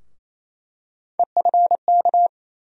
Morse code, heard as a keyed tone, about 28 wpm, 700 Hz. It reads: EFK
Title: USA Call Sign Postfixes - 3 Letters 28wpm